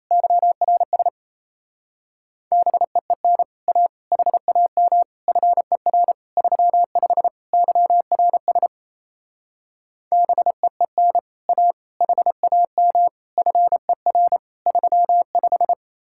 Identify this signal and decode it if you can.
Morse code, heard as a keyed tone, about 36 wpm, 700 Hz. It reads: YRS BEEN A HAM FER 35 YRS BEEN A HAM FER 35